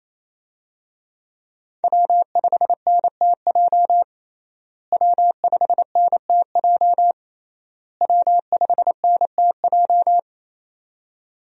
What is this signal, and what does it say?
Morse code, heard as a keyed tone, about 28 wpm, 700 Hz. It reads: W5NTJ W5NTJ W5NTJ
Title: USA Call Signs - 5 Characters - Repeated 3x - 28wpm